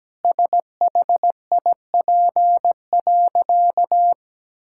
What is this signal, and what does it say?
Morse code, heard as a keyed tone, about 17 wpm, 700 Hz. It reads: SHIP.
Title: Sentences from Top 1000 Words - Review Part 5 - 17wpm